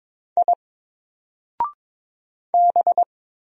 Morse code, heard as a keyed tone, about 22 wpm, 700 Hz. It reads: I B
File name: Single-Letter-Number-22wpm